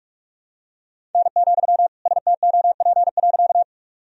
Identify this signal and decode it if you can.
Morse code, heard as a keyed tone, about 41 wpm, 700 Hz. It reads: N, STOP.
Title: Sentences from Top 300 Words and Abbreviations 45wpm